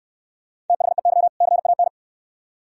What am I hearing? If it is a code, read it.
Morse code, heard as a keyed tone, about 66 wpm, 700 Hz. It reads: THEY CAN